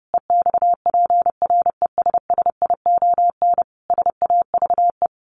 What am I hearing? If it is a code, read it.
Morse code, heard as a keyed tone, about 30 wpm, 700 Hz. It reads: EXPRESSION SAVE
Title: A Sound Of Thunder 30wpm